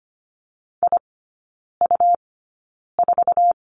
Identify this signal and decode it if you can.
Morse code, heard as a keyed tone, about 25 wpm, spaced as for 9 wpm, 700 Hz. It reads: IU4
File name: Lesson 17e - Letter u - Three Characters 25wpm